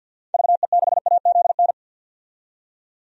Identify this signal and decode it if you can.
Morse code, heard as a keyed tone, about 51 wpm, 700 Hz. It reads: VE6AZN
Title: DXCC-100 Call Signs - 6 Characters - 50wpm